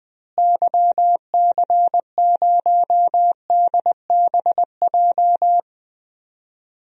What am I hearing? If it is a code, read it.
Morse code, heard as a keyed tone, about 20 wpm, 700 Hz. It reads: YC0DBJ
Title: DXCC-100 Call Signs - 6 Characters - Repeated 2x - 20wpm